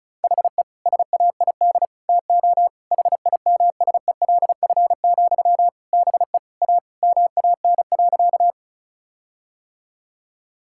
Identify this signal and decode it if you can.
Morse code, heard as a keyed tone, about 36 wpm, 700 Hz. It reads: HE SAID TO HIMSELF, BE A MAN.
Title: Sentences from Top 1600 Words 35wpm